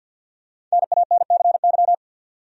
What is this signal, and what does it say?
Morse code, heard as a keyed tone, about 49 wpm, 700 Hz. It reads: NANCY
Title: USA Common Names - Max 5 Letters - 50wpm